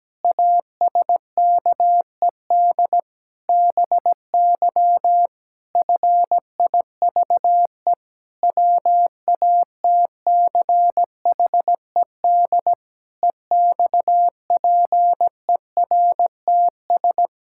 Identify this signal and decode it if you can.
Morse code, heard as a keyed tone, about 17 wpm, 700 Hz. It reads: ASKED BY FIVE WATCHED EXPERTS
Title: Pangram Challenge v2 - 17wpm